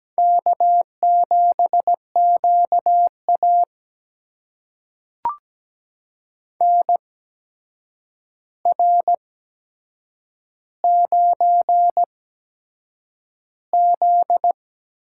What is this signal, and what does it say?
Morse code, heard as a keyed tone, about 17 wpm, spaced as for 7 wpm, 700 Hz. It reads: K7QA NR9Z
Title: USA Call Signs - 4 Characters - 8x Character Spacing - 17wpm